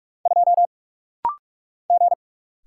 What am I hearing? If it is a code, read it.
Morse code, heard as a keyed tone, about 46 wpm, 700 Hz. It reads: 2 G
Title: Single Letter-Number Alt - Rapid-Fire 45wpm